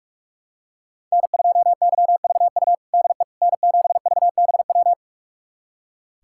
Morse code, heard as a keyed tone, about 46 wpm, 700 Hz. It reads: N2YVU DE N7VBW
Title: USA Callsign DE Callsign 45wpm